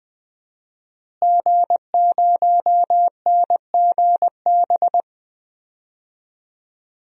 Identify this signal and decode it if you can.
Morse code, heard as a keyed tone, about 20 wpm, 700 Hz. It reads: G0NGB
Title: DXCC-100 Call Signs - 5 Characters - Repeated 3x - 20wpm